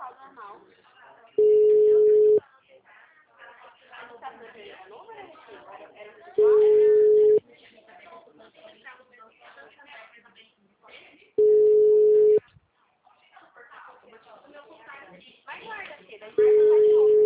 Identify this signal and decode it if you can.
Ringback tone (Europe, 425 Hz), with speech.
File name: out- -2006- - - .32130